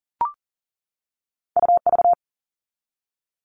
Morse code, heard as a keyed tone, about 40 wpm, 700 Hz. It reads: UV
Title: All Cut Numbers - 2-Digit 40wpm